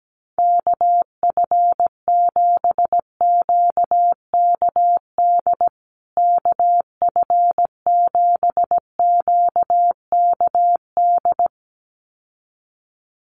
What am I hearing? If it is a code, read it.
Morse code, heard as a keyed tone, about 17 wpm, 700 Hz. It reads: KF7QKD KF7QKD
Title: USA Call Signs Repeated 2x 17wpm